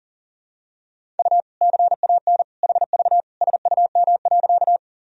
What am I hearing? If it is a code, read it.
Morse code, heard as a keyed tone, about 40 wpm, 700 Hz. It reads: U CAN HV SUM.
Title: Sentences from Top 100 Words and Abbreviations 40wpm